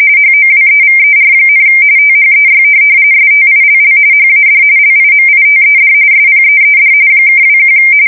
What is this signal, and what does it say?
RTTY, about 45.45 baud. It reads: HOWEVER, THEY STILL HAD LIMITED SPEED (21
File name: fsk170 / USS-New-Jersey